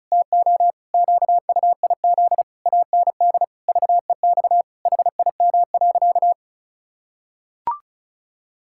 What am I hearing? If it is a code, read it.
Morse code, heard as a keyed tone, about 36 wpm, 700 Hz. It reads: TO QUIZ AND VEX HIM.
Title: Pangram Challenge v2 - 35wpm